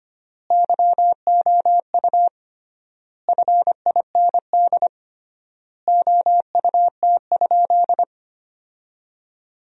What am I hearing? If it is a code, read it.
Morse code, heard as a keyed tone, about 25 wpm, 700 Hz. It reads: YOU FIND OUT?
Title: 3x Word-Spacing - Sentences from Top 200 Words - 25wpm